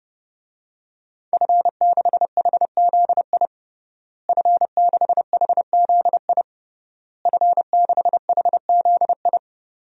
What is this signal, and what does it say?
Morse code, heard as a keyed tone, about 31 wpm, 700 Hz. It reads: F6HZI F6HZI F6HZI
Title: DXCC-100 Call Signs - 5 Characters - Repeated 3x - 30wpm